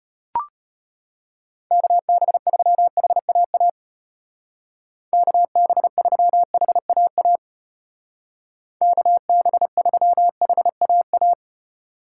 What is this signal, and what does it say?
Morse code, heard as a keyed tone, about 33 wpm, 700 Hz. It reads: KB3HAA KB3HAA KB3HAA